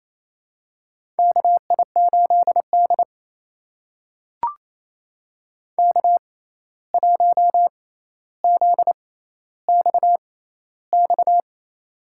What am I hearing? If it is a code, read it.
Morse code, heard as a keyed tone, about 28 wpm, spaced as for 13 wpm, 700 Hz. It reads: KI8D K1ZXX
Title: USA Call Signs - 6x Character Spacing - 28wpm